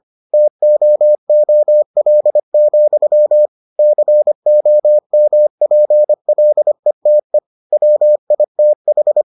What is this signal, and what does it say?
Morse code, heard as a keyed tone, about 25 wpm, 600 Hz. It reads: TOOL, COMPLETE WITH